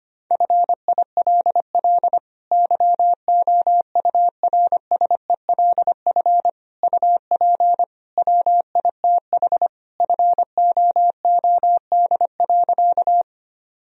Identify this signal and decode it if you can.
Morse code, heard as a keyed tone, about 25 wpm, 700 Hz. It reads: FILL YOURSELF UP WITH FOOD.
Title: Sentences from Top 1000 Words - Review Part 3 - 25wpm